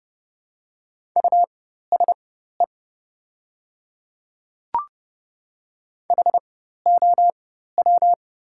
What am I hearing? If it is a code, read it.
Morse code, heard as a keyed tone, about 30 wpm, spaced as for 14 wpm, 700 Hz. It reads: USE HOW
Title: 4x Character Spacing - Top 500 Words - 30wpm